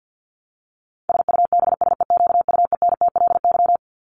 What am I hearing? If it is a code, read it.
Morse code, heard as a keyed tone, about 50 wpm, 700 Hz. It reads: SUBSEQUENTLY
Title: LY Suffix - Top 100 Words with Base Words 50wpm